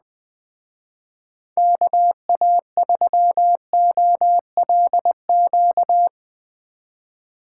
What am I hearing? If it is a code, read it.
Morse code, heard as a keyed tone, about 20 wpm, 700 Hz. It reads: KA3OLQ
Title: Speed Racing - USA 2x3 Call Signs 17wpm